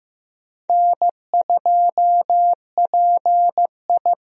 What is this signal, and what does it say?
Morse code, heard as a keyed tone, about 14 wpm, 700 Hz. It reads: N2PI
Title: USA Call Signs - 4 Characters - Repeated 3x - 15wpm